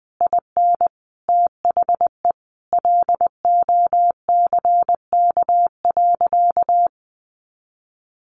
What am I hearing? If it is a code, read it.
Morse code, heard as a keyed tone, about 20 wpm, 700 Hz. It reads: IN THE LOCK.
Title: Sentences from Top 1300 Words 20wpm